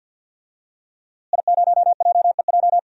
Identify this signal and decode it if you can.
Morse code, heard as a keyed tone, about 48 wpm, 700 Hz. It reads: I0JEJ
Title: DXCC-100 Call Signs - 5 Characters - Repeated 2x - 50wpm